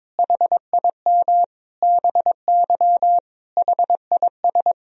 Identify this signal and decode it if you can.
Morse code, heard as a keyed tone, about 22 wpm, 700 Hz. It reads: HIM BY HIS
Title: Sentences from Top 1000 Words - Review Part 1 - 22wpm